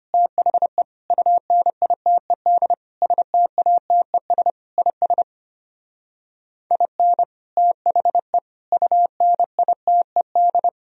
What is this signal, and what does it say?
Morse code, heard as a keyed tone, about 30 wpm, 700 Hz. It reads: THE UNITED STATES IS IN THE UNITED
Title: Speed-Racing - Sets of 5 Words 25wpm